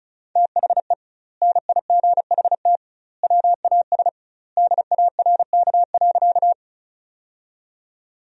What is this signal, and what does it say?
Morse code, heard as a keyed tone, about 36 wpm, 700 Hz. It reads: THE NIGHT WAS DARK.
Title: 2x Word-Spacing - Sentences from Top 500 Words - 35wpm